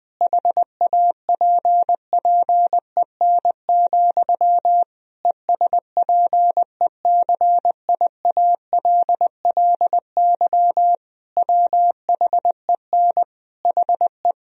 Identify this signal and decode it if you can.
Morse code, heard as a keyed tone, about 21 wpm, 700 Hz. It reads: HAPPEN, ESPECIALLY WHEN HE
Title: Sentences from Top 1000 Words - Review Part 4 - 20wpm